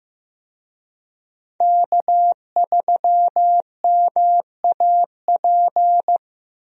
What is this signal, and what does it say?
Morse code, heard as a keyed tone, about 15 wpm, 700 Hz. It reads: K3MAP
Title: USA Call Signs - 5 Characters - Repeated 2x - 15wpm